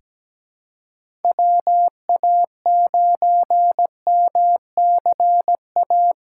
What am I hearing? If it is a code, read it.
Morse code, heard as a keyed tone, about 17 wpm, 700 Hz. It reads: WA9MCA
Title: USA Callsign DE Callsign - 6 Characters - 17wpm